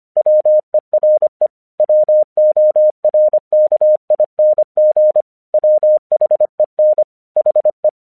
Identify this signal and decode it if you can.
Morse code, heard as a keyed tone, about 26 wpm, 610 Hz. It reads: WERE WORKING WHEN HE
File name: 1 English 25wpm / AQC28